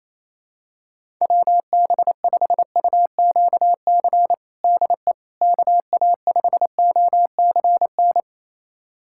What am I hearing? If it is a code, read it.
Morse code, heard as a keyed tone, about 28 wpm, 700 Hz. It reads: WB5UQC DE KA5OCN
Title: USA Callsign DE Callsign - 6 Characters - 28wpm